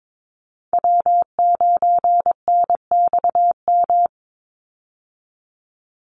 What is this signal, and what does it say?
Morse code, heard as a keyed tone, about 22 wpm, 700 Hz. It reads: W9NXM
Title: Call Signs Repeated 2x - 22wpm